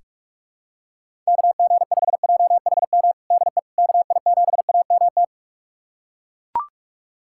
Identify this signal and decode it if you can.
Morse code, heard as a keyed tone, about 41 wpm, 700 Hz. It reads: KG5JHM DE KI7AMT
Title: USA Callsign DE Callsign - 6 Characters - 45wpm